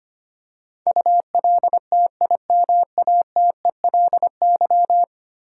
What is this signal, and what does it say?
Morse code, heard as a keyed tone, about 25 wpm, 700 Hz. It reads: ULTIMATELY
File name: Ten Letter Words - 25wpm